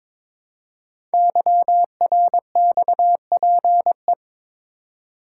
Morse code, heard as a keyed tone, about 22 wpm, 700 Hz. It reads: YRXPE
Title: Code Groups - 5 Letters - 22wpm